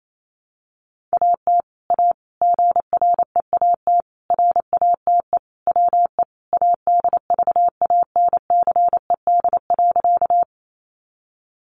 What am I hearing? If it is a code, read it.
Morse code, heard as a keyed tone, about 28 wpm, 700 Hz. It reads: AT A GREAT RATE WE ADVANCED.
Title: Sentences from Top 1200 Words 28wpm